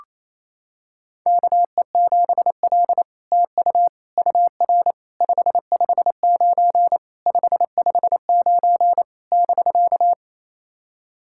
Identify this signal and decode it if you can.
Morse code, heard as a keyed tone, about 27 wpm, 700 Hz. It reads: KE7L TU UR 559 559 <BK>
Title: SOTA QSO Practice 28wpm